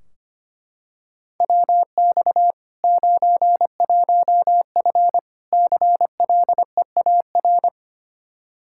Morse code, heard as a keyed tone, about 25 wpm, 700 Hz. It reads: WX 91F CLEAR
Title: Practice QSO - Exchange 2 - Info - 25wpm